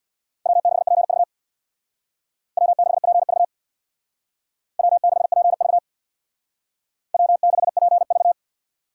Morse code, heard as a keyed tone, about 60 wpm, 700 Hz. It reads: W6PV W6PV W6PV W6PV
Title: Speed Racing - USA 1x2 Call Signs 50wpm